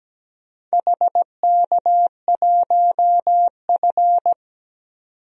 Morse code, heard as a keyed tone, about 17 wpm, 700 Hz. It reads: HK1F
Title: DXCC-100 Call Signs - 4 Characters - 17wpm